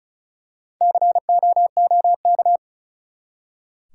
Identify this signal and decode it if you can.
Morse code, heard as a keyed tone, about 36 wpm, 700 Hz. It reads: COOK
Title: Four Letter Words - 35wpm